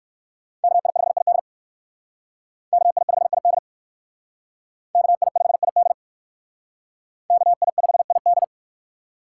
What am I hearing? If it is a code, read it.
Morse code, heard as a keyed tone, about 58 wpm, 700 Hz. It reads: KI5ID KI5ID KI5ID KI5ID